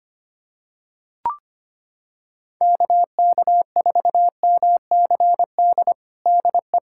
Morse code, heard as a keyed tone, about 25 wpm, 700 Hz. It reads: KK4MCD DE
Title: USA Callsign DE Callsign - 6 Characters - 25wpm